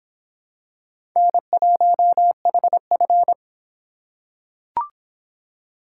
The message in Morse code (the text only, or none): N1HF